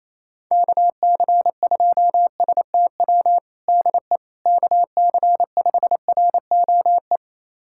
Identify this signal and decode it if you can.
Morse code, heard as a keyed tone, about 28 wpm, 700 Hz. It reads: KC2STW DE KC5ROE
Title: USA Callsign DE Callsign - 6 Characters - 28wpm